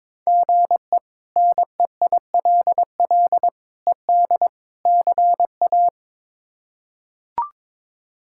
Morse code, heard as a keyed tone, about 22 wpm, 700 Hz. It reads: GE NEILL ED CA
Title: K1USN SST QSO Practice - 22wpm